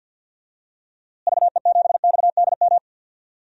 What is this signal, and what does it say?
Morse code, heard as a keyed tone, about 52 wpm, 700 Hz. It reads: VE7XDM
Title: DXCC-100 Call Signs - 6 Characters - Repeated 3x - 50wpm